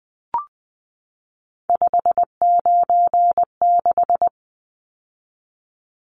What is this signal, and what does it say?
Morse code, heard as a keyed tone, about 20 wpm, 700 Hz. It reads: 596
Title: Numbers - 3-Digit 20wpm